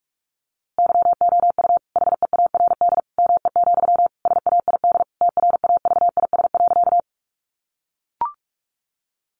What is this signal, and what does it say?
Morse code, heard as a keyed tone, about 44 wpm, 700 Hz. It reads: YOU HEARD ME, SAID TRAVIS.
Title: A Sound Of Thunder 45wpm